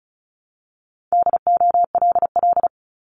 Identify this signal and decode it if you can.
Morse code, heard as a keyed tone, about 36 wpm, 700 Hz. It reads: DOLL
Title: Multi-letter Phonemes 35wpm